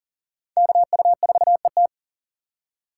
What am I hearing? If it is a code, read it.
Morse code, heard as a keyed tone, about 40 wpm, 700 Hz. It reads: KU4ET
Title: USA Call Signs - 8x Character Spacing - 40wpm